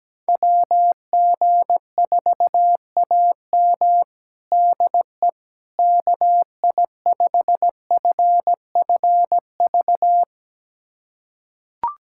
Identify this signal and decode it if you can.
Morse code, heard as a keyed tone, about 17 wpm, 700 Hz. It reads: WG4AM DE KI5FFV
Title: USA Callsign DE Callsign 17wpm